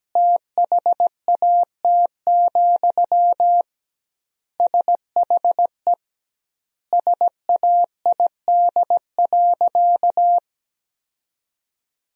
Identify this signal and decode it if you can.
Morse code, heard as a keyed tone, about 17 wpm, 700 Hz. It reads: THAT, SHE SAID.
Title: 2x Word-Spacing - Sentences from Top 200 Words - 17wpm